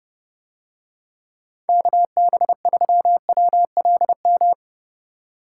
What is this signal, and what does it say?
Morse code, heard as a keyed tone, about 30 wpm, 700 Hz. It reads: KB3WLM